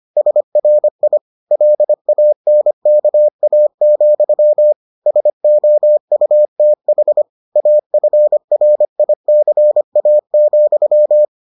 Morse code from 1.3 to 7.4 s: LANKA, SOUTH

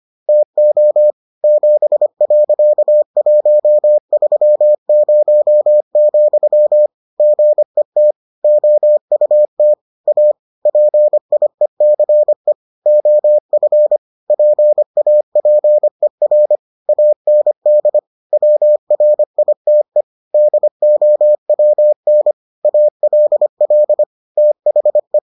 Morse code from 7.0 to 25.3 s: GET OUT A PIECE OF PAPER AND WRITE DOWN ALL THE

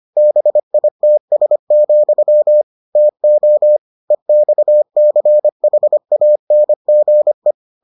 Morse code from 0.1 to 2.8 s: BITS,

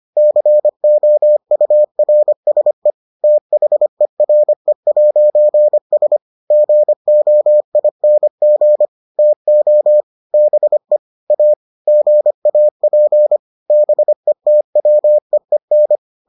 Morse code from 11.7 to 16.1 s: GAP BETWEEN

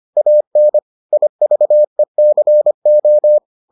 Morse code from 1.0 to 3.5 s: IVECO

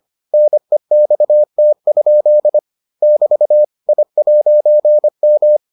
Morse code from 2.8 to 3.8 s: <BT>

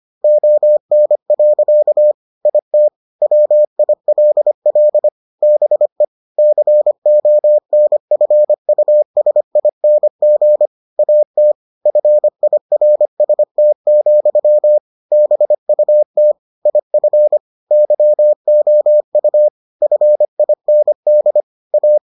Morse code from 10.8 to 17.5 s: AT FIRST, BUT IF

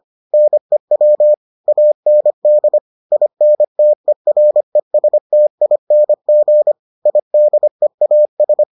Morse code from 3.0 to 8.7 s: INTERESTING IDEAS

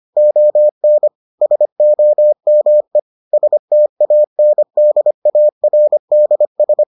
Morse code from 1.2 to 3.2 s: SOME

